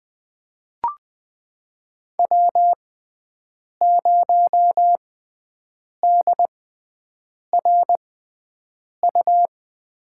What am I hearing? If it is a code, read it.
Morse code, heard as a keyed tone, about 20 wpm, spaced as for 7 wpm, 700 Hz. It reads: W0DRU
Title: USA Call Signs - 5 Characters - 6x Character Spacing - 20wpm